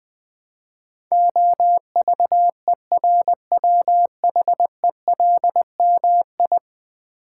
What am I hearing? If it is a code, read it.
Morse code, heard as a keyed tone, about 20 wpm, 700 Hz. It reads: OVERWHELMI
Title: Build A Word - 5 Letters At A Time - 12-Letter Words - 20wpm